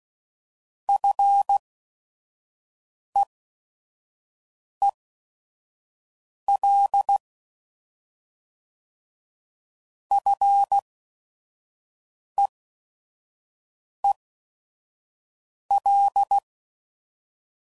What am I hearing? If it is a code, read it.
Morse code, heard as a keyed tone, about 16 wpm, spaced as for 5 wpm, 800 Hz. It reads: FEEL FEEL
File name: Mixed / Short Runs / SinglePrefixes